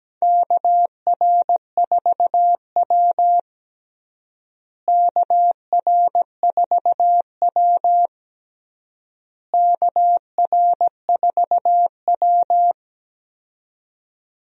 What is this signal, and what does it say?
Morse code, heard as a keyed tone, about 17 wpm, 700 Hz. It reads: KR4W KR4W KR4W
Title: USA Call Signs - 4 Characters - Repeated 3x - 17wpm